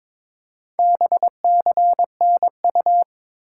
Morse code, heard as a keyed tone, about 22 wpm, 700 Hz. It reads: BCNU
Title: Top 90 QSO Elements 22wpm